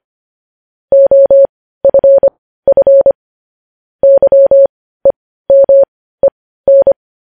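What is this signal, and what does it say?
Morse code, heard as a keyed tone, about 25 wpm, spaced as for 15 wpm, 550 Hz. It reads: OFF YEMEN
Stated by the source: Morse code news headlines sent at 15 words per minute.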